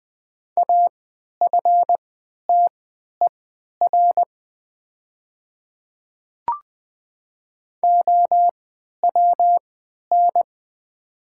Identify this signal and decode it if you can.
Morse code, heard as a keyed tone, about 20 wpm, spaced as for 11 wpm, 700 Hz. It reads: AFTER OWN
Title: 3x Character Spacing - Top 300 Words - 20wpm